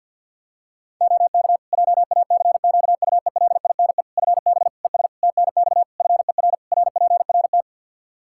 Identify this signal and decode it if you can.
Morse code, heard as a keyed tone, about 47 wpm, 700 Hz. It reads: OK JACQUELINE FB ES TNX FER RPRT
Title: K1USN SST QSO Practice 50wpm